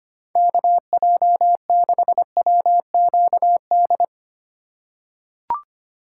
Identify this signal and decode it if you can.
Morse code, heard as a keyed tone, about 25 wpm, 700 Hz. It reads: KJ6WQD
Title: USA Call Signs - 6 Characters - 3x Character Spacing - 25wpm